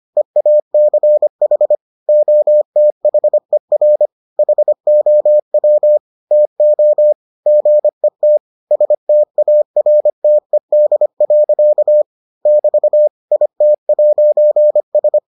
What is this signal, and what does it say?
Morse code, heard as a keyed tone, about 25 wpm, 600 Hz. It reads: EACH OTHER HOW TO GET STARTED. <BT> IT'S